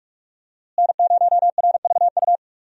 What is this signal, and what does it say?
Morse code, heard as a keyed tone, about 43 wpm, 700 Hz. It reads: N0RVU
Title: USA Call Signs - 5 Characters - Repeated 3x - 45wpm